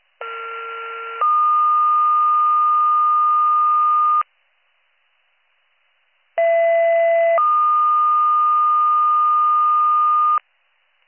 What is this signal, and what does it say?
Two-tone page, heard as a one-second tone, then a three-second tone, about 1370 then 1220 Hz, about 670 then 1220 Hz. It is heard from 0.2 to 10.4 s.